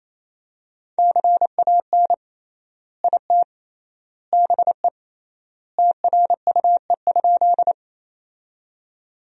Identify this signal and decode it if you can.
Morse code, heard as a keyed tone, about 28 wpm, 700 Hz. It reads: CAN IT BE TRUE?
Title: 3x Word-Spacing - Sentences from Top 400 Words - 28wpm